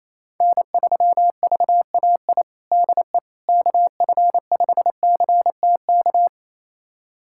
Morse code, heard as a keyed tone, about 28 wpm, 700 Hz. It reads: N3VAI DE KF5CTK
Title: USA Callsign DE Callsign 28wpm